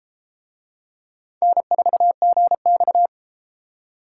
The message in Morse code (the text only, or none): N4GX